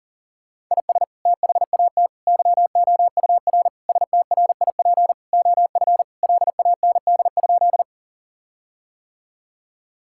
Morse code, heard as a keyed tone, about 40 wpm, 700 Hz. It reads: IS THAT YOUR STRIP OF LAND?
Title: Sentences from Top 1600 Words 40wpm